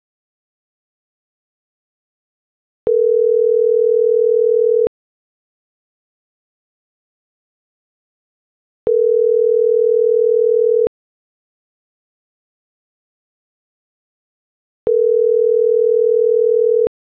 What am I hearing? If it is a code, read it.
Ringback tone (North America), 440 + 480 Hz.